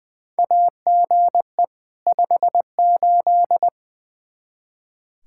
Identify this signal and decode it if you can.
Morse code, heard as a keyed tone, about 20 wpm, 700 Hz. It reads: AGE 58
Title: Practice QSO - Exchange 3 - More Info - 20wpm